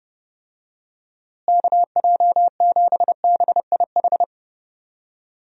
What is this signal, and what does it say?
Morse code, heard as a keyed tone, about 32 wpm, 700 Hz. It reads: KJ7BIH
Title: USA Call Signs - 6 Characters - Repeated 3x - 30wpm